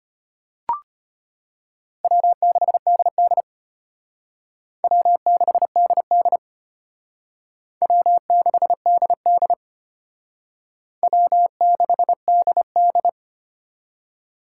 Morse code, heard as a keyed tone, about 33 wpm, 700 Hz. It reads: W6DD W6DD W6DD W6DD